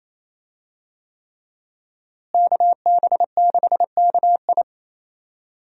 Morse code, heard as a keyed tone, about 28 wpm, 700 Hz. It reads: KB6KI
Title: USA Call Signs - 5 Characters - Repeated 3x - 28wpm